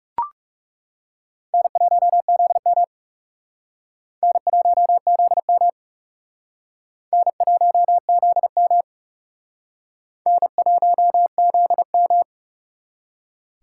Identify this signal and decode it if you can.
Morse code, heard as a keyed tone, about 37 wpm, 700 Hz. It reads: N1ZM N1ZM N1ZM N1ZM
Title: Speed Racing - USA Call Signs 30wpm